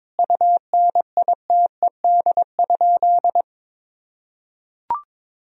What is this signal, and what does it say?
Morse code, heard as a keyed tone, about 21 wpm, 700 Hz. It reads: UNITED?
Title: Sentences from Top 1000 Words - Review Part 4 - 22wpm